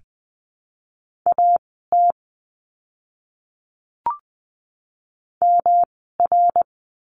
Morse code, heard as a keyed tone, about 20 wpm, spaced as for 15 wpm, 700 Hz. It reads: AT MR
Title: 108 Lesson 18f - Letter m - Words 20wpm